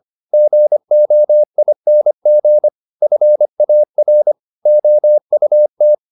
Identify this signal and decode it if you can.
Morse code, heard as a keyed tone, about 25 wpm, 600 Hz. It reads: GOING FAR OUT
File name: the old man and the sea part 1 25wpm